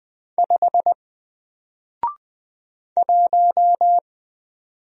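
Morse code, heard as a keyed tone, about 20 wpm, 700 Hz. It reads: 5 1
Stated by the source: Single letter and number Morse drill at 20 wpm, from N4LCD.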